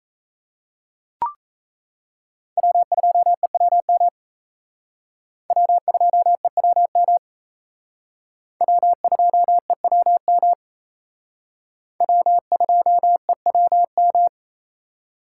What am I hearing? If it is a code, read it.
Morse code, heard as a keyed tone, about 35 wpm, 700 Hz. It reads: W2EWM W2EWM W2EWM W2EWM
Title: Speed Racing - USA 1x3 Call Signs 28wpm